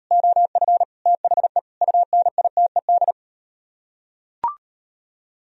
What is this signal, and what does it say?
Morse code, heard as a keyed tone, about 38 wpm, 700 Hz. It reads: OF THE UNITED